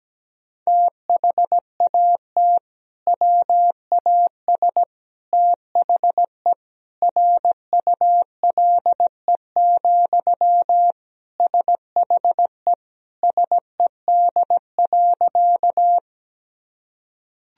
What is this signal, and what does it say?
Morse code, heard as a keyed tone, about 17 wpm, 700 Hz. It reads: THAT WAS THE RULE, SHE SED.
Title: Sentences from Top 400 Words and Abbreviations 17wpm